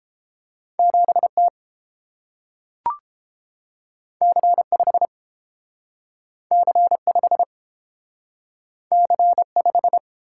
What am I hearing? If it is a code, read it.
Morse code, heard as a keyed tone, about 32 wpm, 700 Hz. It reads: 7 T C 5 C 5 C 5